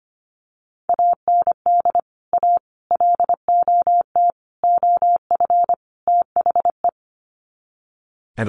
Morse code, heard as a keyed tone, about 25 wpm, 700 Hz. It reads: AND A LOT OF THE
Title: Sets of 5 Words Encore 25wpm